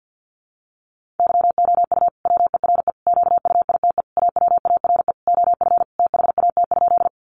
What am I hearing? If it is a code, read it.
Morse code, heard as a keyed tone, about 50 wpm, 700 Hz. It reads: YOU WERE QUITE AWARE OF THAT?
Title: Sentences from Top 1200 Words 50wpm